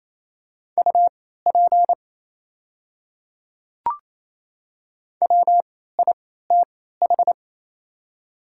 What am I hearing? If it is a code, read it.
Morse code, heard as a keyed tone, about 28 wpm, spaced as for 16 wpm, 700 Hz. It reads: UP WITH
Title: 3x Character Spacing - Top 100 Words - 28wpm